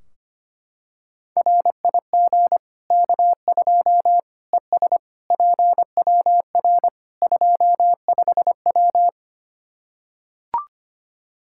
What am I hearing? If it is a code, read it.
Morse code, heard as a keyed tone, about 25 wpm, 700 Hz. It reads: RIG K2 ES PWR 25W
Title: Practice QSO - Exchange 2 - Info - 25wpm